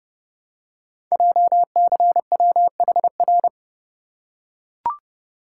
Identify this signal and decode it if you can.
Morse code, heard as a keyed tone, about 30 wpm, 700 Hz. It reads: JCWHR
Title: Code Groups - 5 Letters - 30wpm